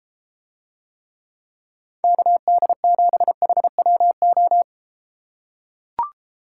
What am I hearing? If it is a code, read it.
Morse code, heard as a keyed tone, about 32 wpm, 700 Hz. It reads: KD7HWO